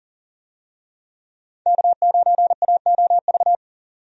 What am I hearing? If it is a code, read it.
Morse code, heard as a keyed tone, about 37 wpm, 700 Hz. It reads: K9AOV
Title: USA Call Signs - 5 Characters - 40wpm